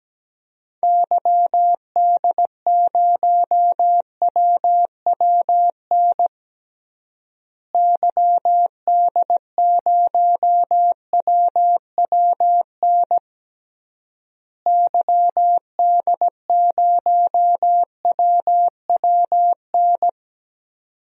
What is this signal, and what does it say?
Morse code, heard as a keyed tone, about 17 wpm, 700 Hz. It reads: YD0WWN YD0WWN YD0WWN
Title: DXCC-100 Call Signs - 6 Characters - Repeated 3x - 17wpm